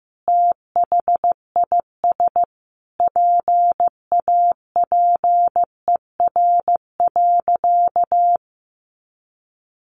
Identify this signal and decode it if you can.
Morse code, heard as a keyed tone, about 14 wpm, 700 Hz. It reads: THIS PAPER.
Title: Sentences from Top 500 Words 15wpm